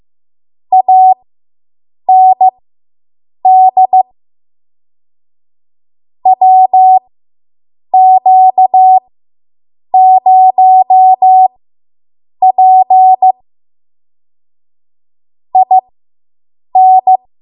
Morse code, heard as a keyed tone, about 15 wpm, spaced as for 7 wpm, 750 Hz. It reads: AND WQ0P IN